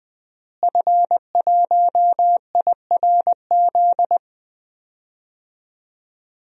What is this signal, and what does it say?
Morse code, heard as a keyed tone, about 20 wpm, 700 Hz. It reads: F1IRZ
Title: DXCC-100 Call Signs - 5 Characters - Repeated 3x - 20wpm